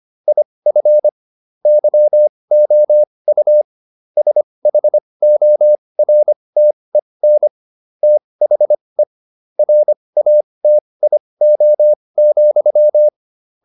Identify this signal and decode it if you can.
Morse code, heard as a keyed tone, about 25 wpm, spaced as for 20 wpm, 600 Hz. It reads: IF YOU SHORTEN THE RATIO,